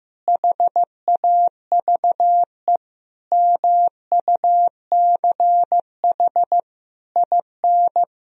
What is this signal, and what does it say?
Morse code, heard as a keyed tone, about 15 wpm, 700 Hz. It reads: HAVE MUCH IN
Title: Sentences from Top 1000 Words - Review Part 3 - 15wpm